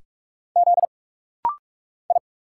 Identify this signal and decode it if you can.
Morse code, heard as a keyed tone, about 46 wpm, 700 Hz. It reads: Z I
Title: Single Letter-Number Alt - Rapid-Fire 45wpm